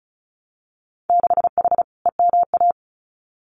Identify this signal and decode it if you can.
Morse code, heard as a keyed tone, about 34 wpm, 700 Hz. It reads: 6H EMA
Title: Winter Field Day Exchanges Encore 35wpm